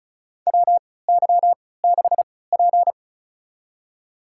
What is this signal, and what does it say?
Morse code, heard as a keyed tone, about 36 wpm, spaced as for 20 wpm, 700 Hz. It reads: WY6P
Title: USA Call Signs - 3x Character Spacing - 35wpm